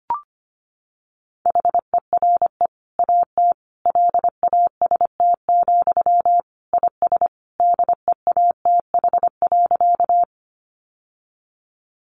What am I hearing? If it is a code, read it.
Morse code, heard as a keyed tone, about 25 wpm, 700 Hz. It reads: HERE AT LAST, IS DEATH.
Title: Sentences from Top 1100 Words 25wpm